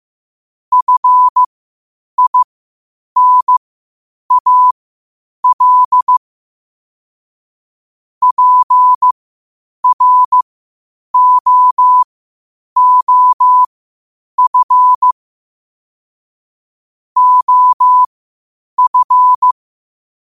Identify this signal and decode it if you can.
Morse code, heard as a keyed tone, about 15 wpm, spaced as for 8 wpm, 1000 Hz. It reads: FINAL PROOF OF
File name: qotdfiles / 8wpm